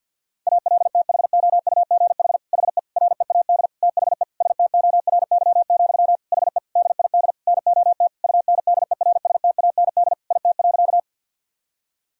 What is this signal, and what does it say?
Morse code, heard as a keyed tone, about 51 wpm, 700 Hz. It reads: ALTHOUGH HE READ THE STORY, HE DID NOT UNDERSTAND IT.
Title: Sentences from Top 1000 Words - Review Part 3 - 50wpm